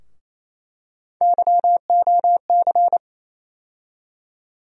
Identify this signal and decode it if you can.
Morse code, heard as a keyed tone, about 28 wpm, 700 Hz. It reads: YOC
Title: USA Call Sign Postfixes - 3 Letters 28wpm